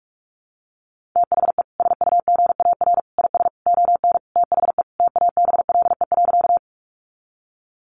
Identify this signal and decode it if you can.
Morse code, heard as a keyed tone, about 44 wpm, 700 Hz. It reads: THE SUGAR IS ON THE TABLE.
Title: Sentences from Top 1200 Words 45wpm